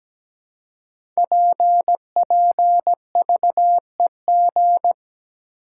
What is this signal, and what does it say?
Morse code, heard as a keyed tone, about 17 wpm, 690 Hz. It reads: PPVEG
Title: Code Groups - 5 Letter:Numbers - 17wpm